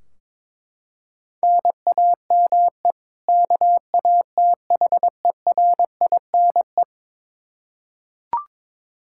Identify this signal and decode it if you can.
Morse code, heard as a keyed tone, about 22 wpm, 700 Hz. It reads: NAME KATHERINE
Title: Practice QSO - Exchange 1 - Beginning - 22wpm